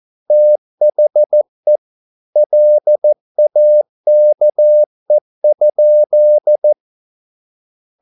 Morse code, heard as a keyed tone, about 14 wpm, 600 Hz. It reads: THE LAKE?